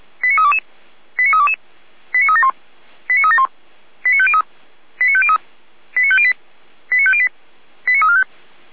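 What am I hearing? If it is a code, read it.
Five-tone selective call (ZVEI), heard as quick runs of five tones, tones of about 70 ms. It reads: 89322 89322 89371 89371 89503 89503 89508 89508 89345
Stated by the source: ZVEI five-tone selective call received by SDR, Feuerwehr Orlamünde.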